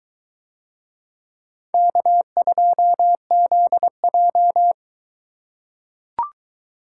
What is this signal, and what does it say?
Morse code, heard as a keyed tone, about 22 wpm, 700 Hz. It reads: K2ZJ